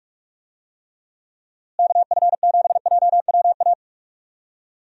Morse code, heard as a keyed tone, about 46 wpm, 700 Hz. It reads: KF7JWA
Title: USA Call Signs - 6 Characters - Repeated 2x - 45wpm